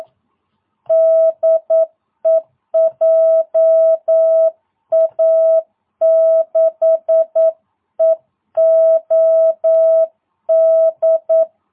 Morse code, heard as a keyed tone, about 9 wpm, 650 Hz. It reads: DE JA6EOD